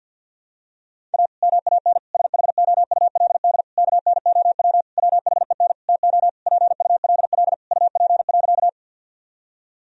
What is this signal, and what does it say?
Morse code, heard as a keyed tone, about 51 wpm, 700 Hz. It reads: A MAN SHOULD KNOW WHEN TO PULL UP.
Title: Sentences from Top 1000 Words - Review Part 2 - 50wpm